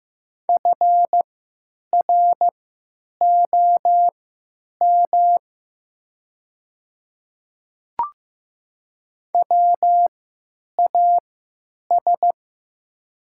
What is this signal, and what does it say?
Morse code, heard as a keyed tone, about 15 wpm, spaced as for 9 wpm, 690 Hz. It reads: FROM WAS
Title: 3x Character Spacing - Top 100 Words - 15wpm